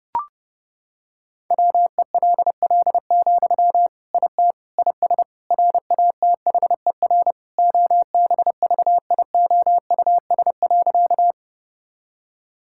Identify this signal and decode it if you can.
Morse code, heard as a keyed tone, about 28 wpm, 700 Hz. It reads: WELL, IT IS RATHER OBVIOUS.
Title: Sentences from Top 1600 Words 30wpm